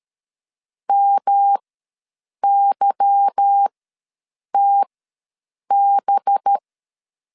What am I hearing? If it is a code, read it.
Morse code, heard as a keyed tone, about 13 wpm, spaced as for 7 wpm, 800 Hz. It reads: MYTB
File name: LEZIONI / 8 YJ45 / 8 Y ws 12wpm 3409